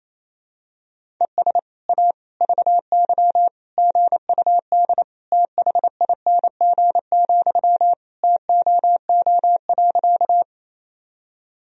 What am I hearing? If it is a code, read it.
Morse code, heard as a keyed tone, about 27 wpm, 700 Hz. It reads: ES A VY GUD THING, TOO.
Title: Sentences from Top 200 Words and Abbreviations 28wpm